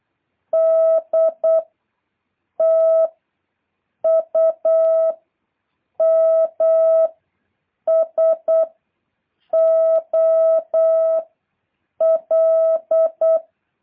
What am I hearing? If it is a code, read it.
Morse code, heard as a keyed tone, about 8 wpm, 650 Hz. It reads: D T U M S O L
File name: jh8cbh / 2023cw